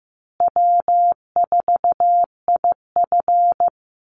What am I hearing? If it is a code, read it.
Morse code, heard as a keyed tone, about 15 wpm, 700 Hz. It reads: W4IF
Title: Easy Call Signs Repeated 3x - 15wpm